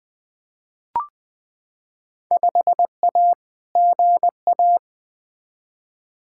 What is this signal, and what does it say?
Morse code, heard as a keyed tone, about 20 wpm, 700 Hz. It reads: 5A GA
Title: Field Day Exchanges Encore - 20wpm